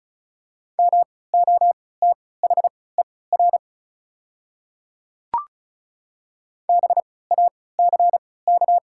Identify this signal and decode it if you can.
Morse code, heard as a keyed tone, about 36 wpm, spaced as for 20 wpm, 690 Hz. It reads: MOTHER BACK
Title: 3x Character Spacing - Top 500 Words - 35wpm